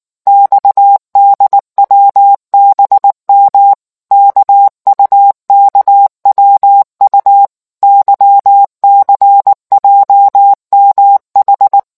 Morse code, heard as a keyed tone, about 19 wpm, 800 Hz. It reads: XDWBM KUKWU YCJMH